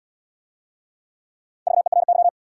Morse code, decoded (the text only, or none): VERY